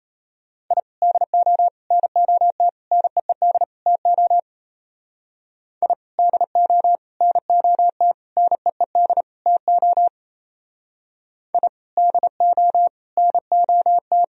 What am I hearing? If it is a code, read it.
Morse code, heard as a keyed tone, about 31 wpm, 700 Hz. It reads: I DO NOT NEED TO I DO NOT NEED TO I DO NOT
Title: Speed-Racing - Sets of 5 Words 28wpm